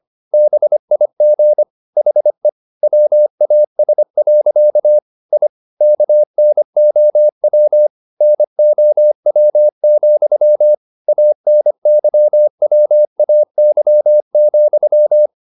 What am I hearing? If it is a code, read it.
Morse code, heard as a keyed tone, about 25 wpm, 600 Hz. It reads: BIG HE WAS. I KNOW NOW, ANYWAY,
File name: the old man and the sea part 1 25wpm